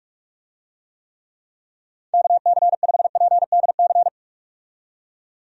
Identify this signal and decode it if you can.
Morse code, heard as a keyed tone, about 46 wpm, 700 Hz. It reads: KC5PDC
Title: USA Call Signs - 6 Characters - 45wpm